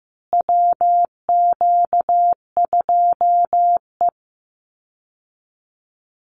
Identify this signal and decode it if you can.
Morse code, heard as a keyed tone, about 15 wpm, 700 Hz. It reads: WQ2E
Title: Call Signs Repeated 2x - 15wpm